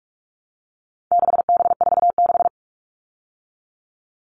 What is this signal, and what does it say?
Morse code, heard as a keyed tone, about 44 wpm, 700 Hz. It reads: 6B46
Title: All Cut Numbers - 4-Digit 45wpm